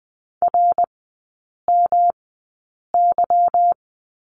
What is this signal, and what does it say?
Morse code, heard as a keyed tone, about 20 wpm, spaced as for 8 wpm, 700 Hz. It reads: RMY
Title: 145 Lesson 24e - Letter y - Three Characters 20wpm